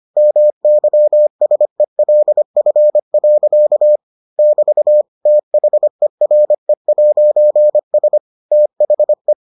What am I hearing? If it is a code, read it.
Morse code, heard as a keyed tone, about 25 wpm, 600 Hz. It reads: MYSELF. <BT> THERE'S THE